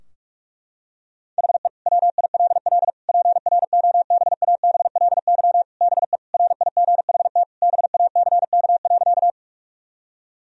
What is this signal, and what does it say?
Morse code, heard as a keyed tone, about 46 wpm, 700 Hz. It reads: HE WILL PROBABLY BE RIGHT BACK.
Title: Sentences from Top 1000 Words - Review Part 4 - 45wpm